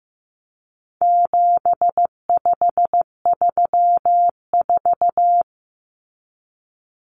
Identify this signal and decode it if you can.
Morse code, heard as a keyed tone, about 15 wpm, 700 Hz. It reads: 7534
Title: Common Cut Numbers - 4-Digit 15wpm